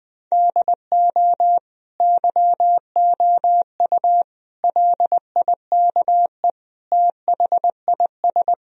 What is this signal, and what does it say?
Morse code, heard as a keyed tone, about 20 wpm, 700 Hz. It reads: DO YOU LIKE THIS
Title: Sentences from Top 1000 Words - Review Part 4 - 20wpm